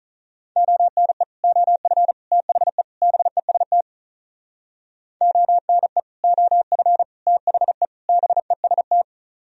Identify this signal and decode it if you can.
Morse code, heard as a keyed tone, about 46 wpm, 700 Hz. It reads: ONE OF THE BEST ONE OF THE BEST